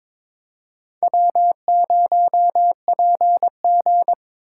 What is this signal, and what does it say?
Morse code, heard as a keyed tone, about 22 wpm, 700 Hz. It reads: W0PG
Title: USA Call Signs - 4 Characters - 6x Character Spacing - 22wpm